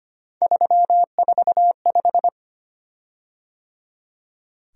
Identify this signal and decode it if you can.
Morse code, heard as a keyed tone, about 25 wpm, 700 Hz. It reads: 345
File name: Three Letter-Number Combinations - 25wpm